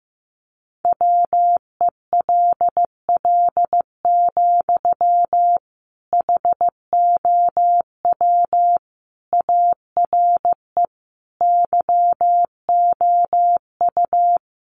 Morse code, heard as a keyed tone, about 15 wpm, 700 Hz. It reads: WELL, HOW ARE YOU
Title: Sentences from Top 1200 Words 15wpm